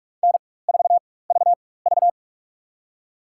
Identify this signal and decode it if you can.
Morse code, heard as a keyed tone, about 42 wpm, spaced as for 20 wpm, 700 Hz. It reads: N4VV
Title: USA Call Signs - 4 Characters - 4x Character Spacing - 45wpm